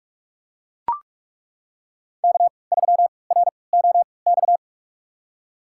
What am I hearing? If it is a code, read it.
Morse code, heard as a keyed tone, about 41 wpm, spaced as for 25 wpm, 700 Hz. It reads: K3ROX
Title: USA Call Signs - 3x Character Spacing - 45wpm